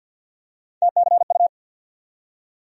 Morse code, heard as a keyed tone, about 48 wpm, 700 Hz. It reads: TCU
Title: USA Call Sign Postfixes - 3 Letters 50wpm